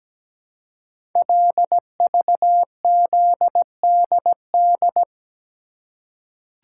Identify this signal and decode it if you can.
Morse code, heard as a keyed tone, about 17 wpm, 700 Hz. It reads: LVZDD
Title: Code Groups - 5 Letters - 17wpm